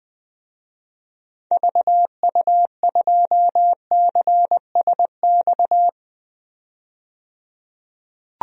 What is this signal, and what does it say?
Morse code, heard as a keyed tone, about 20 wpm, 700 Hz. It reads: VU2CSX
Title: DXCC-100 Call Signs - 6 Characters - 20wpm